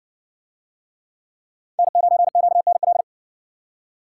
Morse code, heard as a keyed tone, about 60 wpm, 700 Hz. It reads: N9QNL